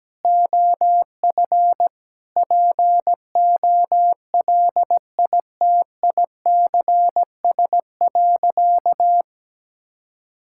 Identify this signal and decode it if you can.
Morse code, heard as a keyed tone, about 17 wpm, 700 Hz. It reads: OF POLITICS.
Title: Sentences from Top 1000 Words - Review Part 6 - 17wpm